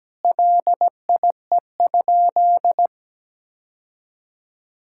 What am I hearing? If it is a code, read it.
Morse code, heard as a keyed tone, about 17 wpm, 700 Hz. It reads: LIE?
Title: Sentences from Top 1600 Words 17wpm